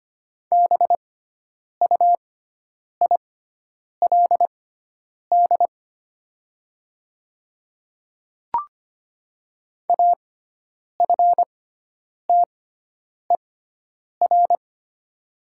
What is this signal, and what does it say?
Morse code, heard as a keyed tone, about 25 wpm, spaced as for 9 wpm, 700 Hz. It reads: BUILD AFTER
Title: 6x Character Spacing - Top 200 Words - 25wpm